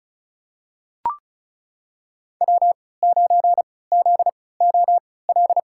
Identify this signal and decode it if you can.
Morse code, heard as a keyed tone, about 36 wpm, spaced as for 20 wpm, 700 Hz. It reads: W9ZOL
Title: USA Call Signs - 5 Characters - 3x Character Spacing - 35wpm